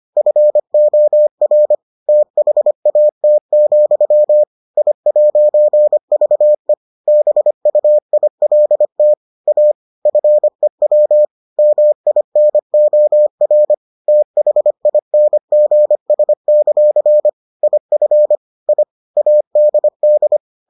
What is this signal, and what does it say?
Morse code, heard as a keyed tone, about 25 wpm, 600 Hz. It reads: FOR THAT, I'VE BUILT A FEW MINOR THINGS; IF I ADD